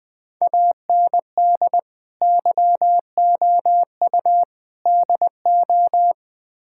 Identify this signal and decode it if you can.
Morse code, heard as a keyed tone, about 21 wpm, 690 Hz. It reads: AND YOU DO
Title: Build A Phrase - 4-Word Phrases - 20wpm